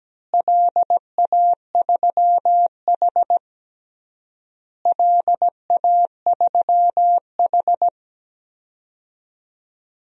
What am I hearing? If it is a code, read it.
Morse code, heard as a keyed tone, about 17 wpm, 700 Hz. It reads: LA3H LA3H
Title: DXCC-100 Call Signs - 4 Characters - Repeated 3x - 17wpm